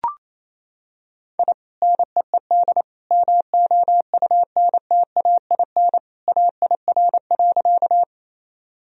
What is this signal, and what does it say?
Morse code, heard as a keyed tone, about 28 wpm, 700 Hz. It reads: I NEED MOUNTAIN AIR.
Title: Sentences from Top 1000 Words - Review Part 1 - 28wpm